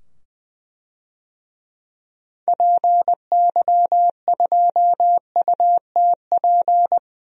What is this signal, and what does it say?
Morse code, heard as a keyed tone, about 21 wpm, 700 Hz. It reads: PY2UTP
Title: DXCC-100 Call Signs - 6 Characters - 20wpm